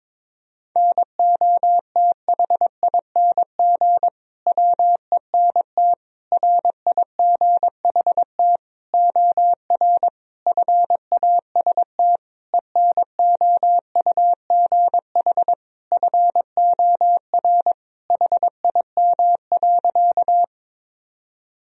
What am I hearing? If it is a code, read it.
Morse code, heard as a keyed tone, about 22 wpm, 700 Hz. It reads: NOTHING WENT RIGHT OR FAST ENOUGH FOR HIM.
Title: Sentences from Top 1000 Words - Review Part 2 - 22wpm